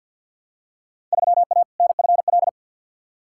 Morse code, heard as a keyed tone, about 48 wpm, 700 Hz. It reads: 3A NFL
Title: Field Day Exchanges - 50wpm